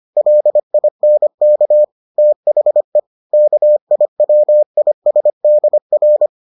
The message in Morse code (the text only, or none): LINK THE KIWISDR